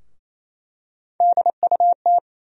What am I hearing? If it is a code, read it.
Morse code, heard as a keyed tone, about 28 wpm, 700 Hz. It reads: DUT
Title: USA Call Sign Postfixes - 3 Letters 28wpm